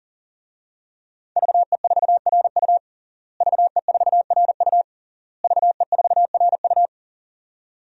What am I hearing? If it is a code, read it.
Morse code, heard as a keyed tone, about 40 wpm, 700 Hz. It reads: VE4RU VE4RU VE4RU
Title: DXCC-100 Call Signs - 5 Characters - Repeated 3x - 40wpm